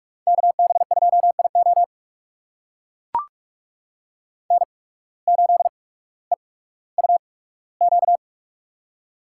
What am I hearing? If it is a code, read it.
Morse code, heard as a keyed tone, about 44 wpm, spaced as for 19 wpm, 700 Hz. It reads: KB2IO N8EUQ
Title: USA Call Signs - 5 Characters - 8x Character Spacing - 45wpm